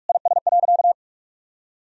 Morse code, heard as a keyed tone, about 45 wpm, 700 Hz. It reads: IS.
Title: Sentences from Top 1000 Words - Review Part 2 - 45wpm